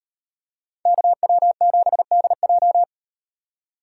Morse code, heard as a keyed tone, about 38 wpm, 700 Hz. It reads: KW7DJ